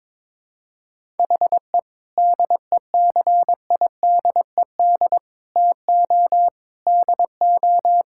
Morse code, heard as a keyed tone, about 22 wpm, 700 Hz. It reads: HE DECIDED TO DO
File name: practice / 22wpm / Sentences with 3-Word Phrases - 22wpm